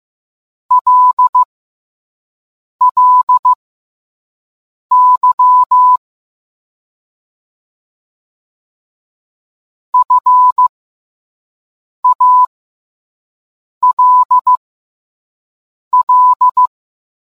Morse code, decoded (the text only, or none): LLY FALL